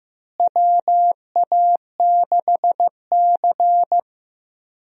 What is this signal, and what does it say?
Morse code, heard as a keyed tone, about 15 wpm, 700 Hz. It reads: WA6C
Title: USA Call Signs - 4 Characters - Repeated 3x - 15wpm